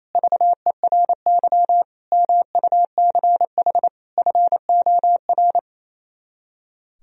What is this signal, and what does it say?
Morse code, heard as a keyed tone, about 27 wpm, 700 Hz. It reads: VERY MUCH FOR
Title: Speed-Racing - Sets of 4 Words 28wpm